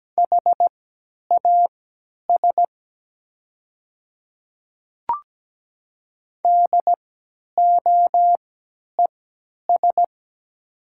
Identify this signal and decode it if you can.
Morse code, heard as a keyed tone, about 17 wpm, spaced as for 10 wpm, 700 Hz. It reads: HAS DOES
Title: 3x Character Spacing - Top 200 Words - 17wpm